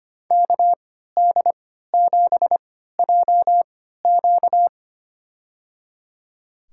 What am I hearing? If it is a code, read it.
Morse code, heard as a keyed tone, about 25 wpm, spaced as for 14 wpm, 700 Hz. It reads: KD7JQ
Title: USA Call Signs - 5 Characters - 3x Character Spacing - 25wpm